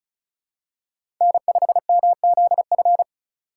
Morse code, heard as a keyed tone, about 36 wpm, 700 Hz. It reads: N5MZF
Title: USA Call Signs - 5 Characters - 2x Character Spacing - 35wpm